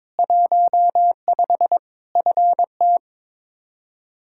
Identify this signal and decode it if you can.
Morse code, heard as a keyed tone, about 22 wpm, 700 Hz. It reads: 15 FT
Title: Practice QSO - Exchange 2 - Info - 22wpm